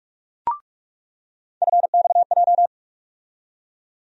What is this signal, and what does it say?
Morse code, heard as a keyed tone, about 46 wpm, 700 Hz. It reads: FXJ
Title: USA Call Sign Postfixes - 3 Letters 45wpm